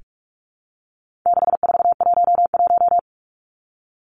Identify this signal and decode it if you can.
Morse code, heard as a keyed tone, about 44 wpm, 700 Hz. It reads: 6411
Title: Numbers - 4-Digit 45wpm